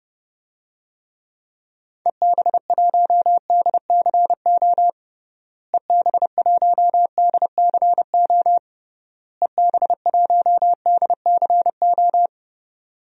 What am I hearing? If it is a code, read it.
Morse code, heard as a keyed tone, about 30 wpm, 700 Hz. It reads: EB1DCO EB1DCO EB1DCO
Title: DXCC-100 Call Signs - 6 Characters - Repeated 3x - 30wpm